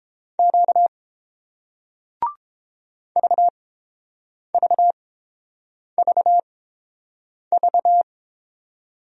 Morse code, heard as a keyed tone, about 30 wpm, spaced as for 7 wpm, 700 Hz. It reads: Q VVVV